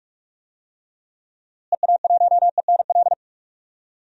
Morse code, heard as a keyed tone, about 46 wpm, 700 Hz. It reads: EA1ENL
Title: DXCC-100 Call Signs - 6 Characters - Repeated 3x - 45wpm